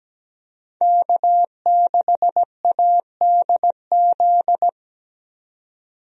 Morse code, heard as a keyed tone, about 17 wpm, 700 Hz. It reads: K6ADZ
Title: USA Call Signs - 5 Characters - 17wpm